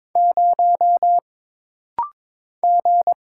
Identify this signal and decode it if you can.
Morse code, heard as a keyed tone, about 22 wpm, 700 Hz. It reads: 0 G
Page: This rapid-fire timing is geared to push your ICR (Instant Character Recognition) skills.